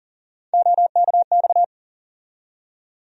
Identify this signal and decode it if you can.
Morse code, heard as a keyed tone, about 40 wpm, 700 Hz. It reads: OKX
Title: USA Call Sign Postfixes - 3 Letters 40wpm